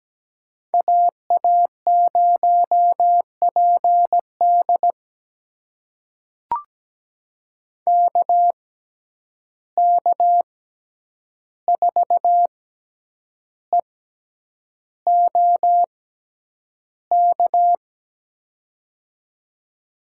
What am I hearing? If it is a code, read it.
Morse code, heard as a keyed tone, about 17 wpm, spaced as for 8 wpm, 700 Hz. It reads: AA0PD KK4EOK
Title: USA Call Signs - 6x Character Spacing - 17wpm